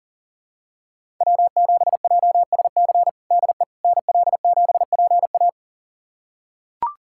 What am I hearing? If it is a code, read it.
Morse code, heard as a keyed tone, about 40 wpm, 700 Hz. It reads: W7JSC DE NL7PA
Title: USA Callsign DE Callsign - 5 Characters - 40wpm